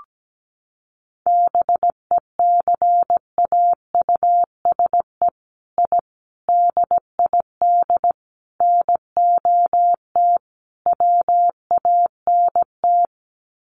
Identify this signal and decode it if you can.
Morse code, heard as a keyed tone, about 17 wpm, 700 Hz. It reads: BECAUSE I DID NOT WANT
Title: Sets of 5 Words Encore2 17wpm